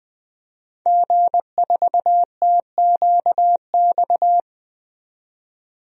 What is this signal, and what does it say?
Morse code, heard as a keyed tone, about 20 wpm, 700 Hz. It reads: G4TQX
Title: DXCC-100 Call Signs - 5 Characters - 20wpm